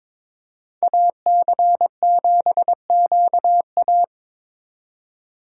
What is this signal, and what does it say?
Morse code, heard as a keyed tone, about 26 wpm, 700 Hz. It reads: AC7QA
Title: Speed Racing - USA 2x2 Call Signs 22wpm